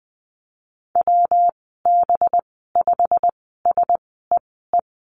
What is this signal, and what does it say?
Morse code, heard as a keyed tone, about 20 wpm, spaced as for 14 wpm, 700 Hz. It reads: WB5SEE
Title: 193 Lesson 31g - Letter v - Call Signs 20wpm